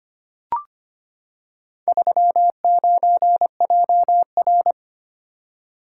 Morse code, heard as a keyed tone, about 25 wpm, 700 Hz. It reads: 39JR
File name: practice / 25wpm / Four Letter-Number Combinations - 25wpm